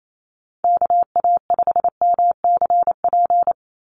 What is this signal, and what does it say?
Morse code, heard as a keyed tone, about 28 wpm, 700 Hz. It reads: KA5MCP
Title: USA Call Signs 28wpm